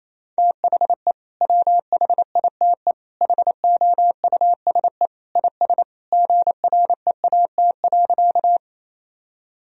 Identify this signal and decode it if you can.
Morse code, heard as a keyed tone, about 28 wpm, 700 Hz. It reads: THE WHITE HOUSE IS GREAT.
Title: Sentences from Top 1000 Words - Review Part 1 - 28wpm